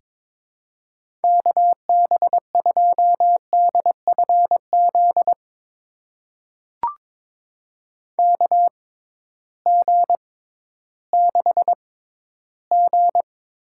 Morse code, heard as a keyed tone, about 22 wpm, spaced as for 13 wpm, 700 Hz. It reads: KB2DFZ KG6G
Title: USA Call Signs - 6 Characters - 6x Character Spacing - 22wpm